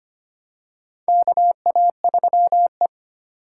Morse code, heard as a keyed tone, about 25 wpm, 700 Hz. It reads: KA3E
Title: USA Call Signs - 2x Character Spacing - 25wpm